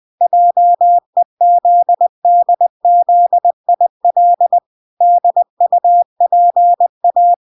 Morse code, heard as a keyed tone, about 20 wpm, 700 Hz. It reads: JEZDZIL DUPA